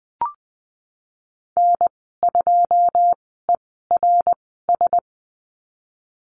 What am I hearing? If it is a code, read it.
Morse code, heard as a keyed tone, about 20 wpm, spaced as for 14 wpm, 700 Hz. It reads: N2ERS
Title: 088 Lesson 15g - Number 5 - Call Signs 20wpm